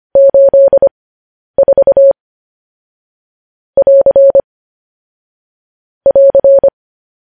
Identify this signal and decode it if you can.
Morse code, heard as a keyed tone, about 25 wpm, spaced as for 10 wpm, 550 Hz. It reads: 84 <AR> <AR>